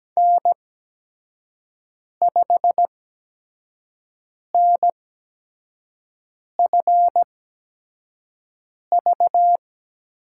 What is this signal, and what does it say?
Morse code, heard as a keyed tone, about 17 wpm, spaced as for 5 wpm, 700 Hz. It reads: N5NFV
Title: USA Call Signs - 5 Characters - 8x Character Spacing - 17wpm